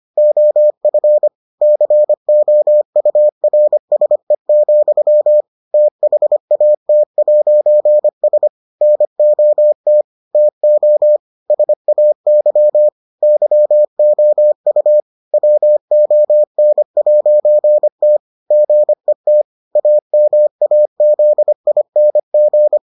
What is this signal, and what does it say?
Morse code, heard as a keyed tone, about 25 wpm, 600 Hz. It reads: OF COURSE, THAT'S NOT TO SAY YOU WON'T GET AMAZING